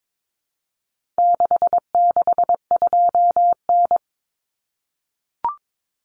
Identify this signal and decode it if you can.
Morse code, heard as a keyed tone, about 22 wpm, 700 Hz. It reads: 662N
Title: Common Cut Numbers - 4-Digit 22wpm